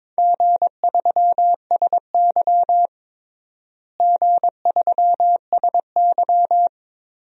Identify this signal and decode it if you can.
Morse code, heard as a keyed tone, about 22 wpm, 700 Hz. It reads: G3SY G3SY
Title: DXCC-100 Call Signs - 4 Characters - Repeated 2x - 22wpm